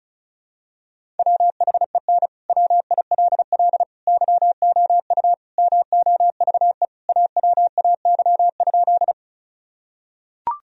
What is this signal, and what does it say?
Morse code, heard as a keyed tone, about 34 wpm, 700 Hz. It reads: WHEN WILL YOU MOVE AWAY?
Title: Sentences from Top 1000 Words - Review Part 1 - 35wpm